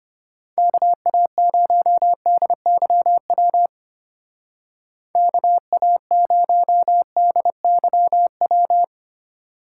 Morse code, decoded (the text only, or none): KA0DYW KA0DYW